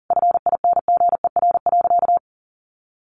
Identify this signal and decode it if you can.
Morse code, heard as a keyed tone, about 40 wpm, 700 Hz. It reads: FINGER.
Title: Additional Easy Phrases 40wpm